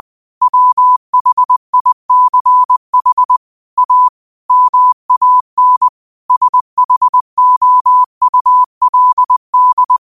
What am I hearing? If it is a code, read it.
Morse code, heard as a keyed tone, about 20 wpm, 1000 Hz. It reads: WHICH A MAN SHOULD